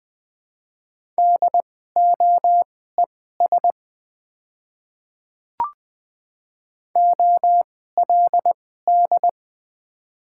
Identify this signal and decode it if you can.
Morse code, heard as a keyed tone, about 20 wpm, spaced as for 15 wpm, 700 Hz. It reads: DOES OLD
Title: 2x Character Spacing - Top 200 Words - 20wpm